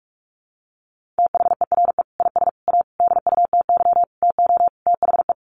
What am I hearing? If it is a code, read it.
Morse code, heard as a keyed tone, about 44 wpm, 700 Hz. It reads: THERE IS A DUTY TO THE
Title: Sentences from Top 1500 Words 45wpm